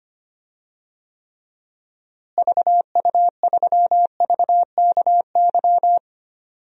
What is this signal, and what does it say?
Morse code, heard as a keyed tone, about 25 wpm, 700 Hz. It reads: VU3VKY
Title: DXCC-100 Call Signs - 6 Characters - Repeated 3x - 25wpm